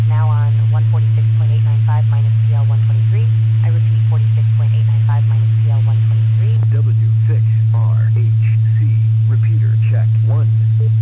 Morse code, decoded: none

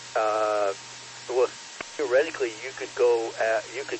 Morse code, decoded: none

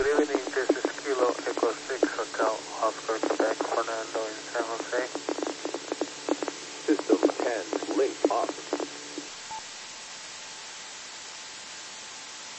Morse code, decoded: none

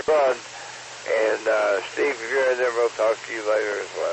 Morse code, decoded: none

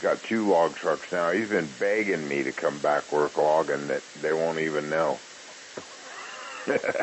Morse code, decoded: none

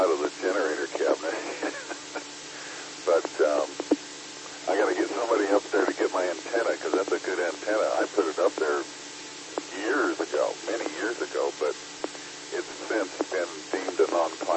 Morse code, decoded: none

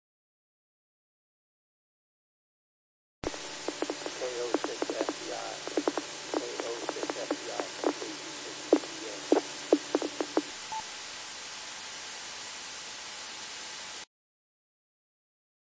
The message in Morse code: none

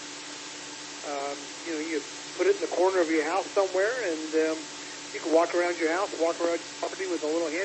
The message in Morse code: none

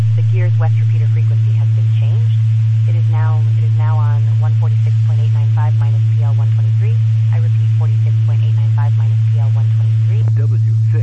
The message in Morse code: none

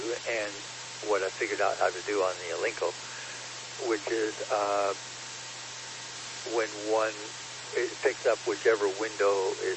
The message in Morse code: none

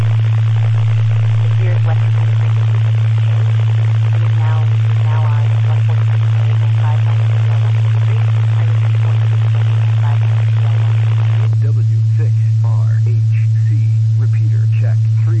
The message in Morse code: none

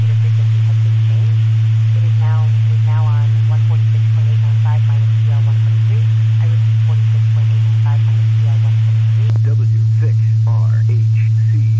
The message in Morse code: none

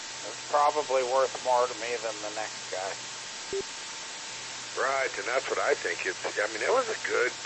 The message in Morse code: none

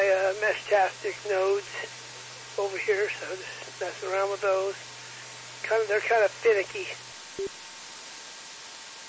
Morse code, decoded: none

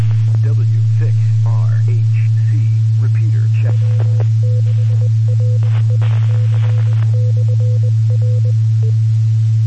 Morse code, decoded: none

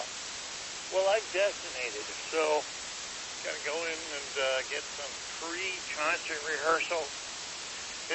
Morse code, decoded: none

